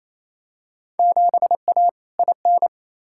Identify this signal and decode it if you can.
Morse code, heard as a keyed tone, about 27 wpm, 700 Hz. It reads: 7A IN
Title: Field Day Exchanges Encore - 28wpm